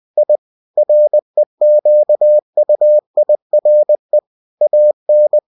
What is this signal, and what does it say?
Morse code, heard as a keyed tone, about 20 wpm, 600 Hz. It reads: I REQUIRE AN
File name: cw-books / winnie-the-pooh / winnie-20wpm-page-0029